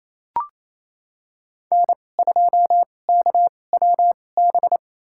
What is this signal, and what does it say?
Morse code, heard as a keyed tone, about 28 wpm, spaced as for 20 wpm, 700 Hz. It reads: N2KWB
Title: USA Call Signs - 5 Characters - 2x Character Spacing - 28wpm